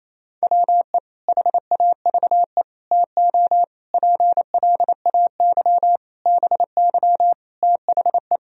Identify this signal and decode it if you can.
Morse code, heard as a keyed tone, about 28 wpm, 700 Hz. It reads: WE HAVE TO PLAY BY THE
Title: Sentences from Top 1000 Words - Review Part 3 - 28wpm